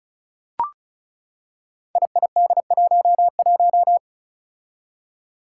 Morse code, heard as a keyed tone, about 36 wpm, 700 Hz. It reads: IID11
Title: Five Challenging Letter-Number Combinations 35wpm